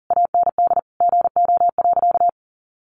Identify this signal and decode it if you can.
Morse code, heard as a keyed tone, about 40 wpm, 700 Hz. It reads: AND GO.
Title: Sentences from Top 1200 Words 40wpm